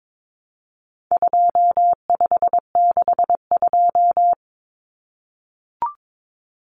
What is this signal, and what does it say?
Morse code, heard as a keyed tone, about 22 wpm, 700 Hz. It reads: 2562
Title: Common Cut Numbers - 4-Digit 22wpm